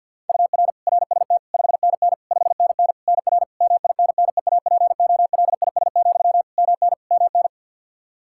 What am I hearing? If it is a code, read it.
Morse code, heard as a keyed tone, about 51 wpm, 700 Hz. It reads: UR RST 5NN 5NN NR MINNEAPOLIS, MN MN
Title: Practice QSO - Exchange 1 - Beginning - 50wpm